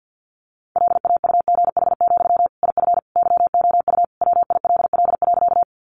Morse code, heard as a keyed tone, about 50 wpm, 700 Hz. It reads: LAUGH, IF YOU WILL.
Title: Sentences from Top 500 Words 50wpm